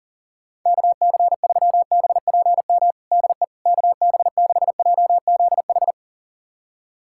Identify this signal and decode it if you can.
Morse code, heard as a keyed tone, about 37 wpm, 700 Hz. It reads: KC3BPM DE KB6JZH
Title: USA Callsign DE Callsign - 6 Characters - 40wpm